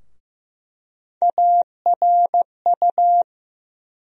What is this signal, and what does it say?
Morse code, heard as a keyed tone, about 15 wpm, 700 Hz. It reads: ARU
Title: USA Call Sign Postfixes - 3 Letters 15wpm